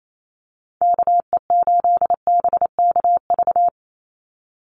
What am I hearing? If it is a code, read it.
Morse code, heard as a keyed tone, about 28 wpm, 700 Hz. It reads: KE8BKV
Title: USA Call Signs 28wpm